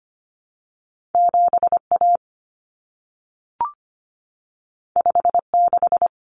Morse code, heard as a keyed tone, about 25 wpm, 700 Hz. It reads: 7A 56
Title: Common Cut Numbers - 2-Digit 25wpm